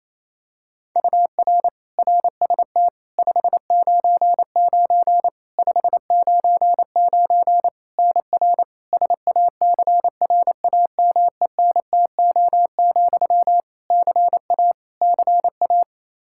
Morse code, heard as a keyed tone, about 28 wpm, 700 Hz. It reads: UR RST 599 599 NR SACRAMENTO, CA CA
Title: K1USN SST QSO Practice 28wpm